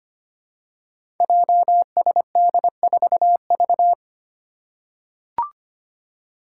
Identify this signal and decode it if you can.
Morse code, heard as a keyed tone, about 25 wpm, 700 Hz. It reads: JSD4V
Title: Five Challenging Letter-Number Combinations 25wpm